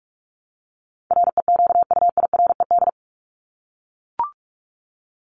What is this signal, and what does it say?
Morse code, heard as a keyed tone, about 44 wpm, 700 Hz. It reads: REQUIRED
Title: ED Suffix - Top 100 Words with Base Words 45wpm